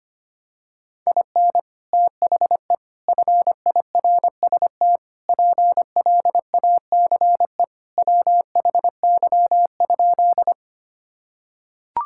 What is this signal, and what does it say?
Morse code, heard as a keyed tone, about 25 wpm, 700 Hz. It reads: IN THE FIRST PLACE WHY?
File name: practice / 25wpm / Sentences with 4-Word Phrases - 25wpm